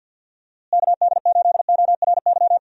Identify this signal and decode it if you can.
Morse code, heard as a keyed tone, about 47 wpm, 700 Hz. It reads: KD8ORY
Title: USA Call Signs - 6 Characters - Repeated 2x - 50wpm